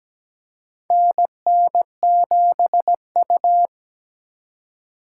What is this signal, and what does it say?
Morse code, heard as a keyed tone, about 17 wpm, 700 Hz. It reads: NN7U
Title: USA Call Signs - 4 Characters - 2x Character Spacing - 17wpm